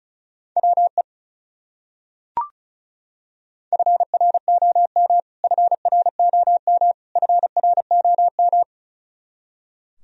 Morse code, heard as a keyed tone, about 34 wpm, 700 Hz. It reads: WE FROM FROM FROM
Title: Top 100 Words Repeated 35wpm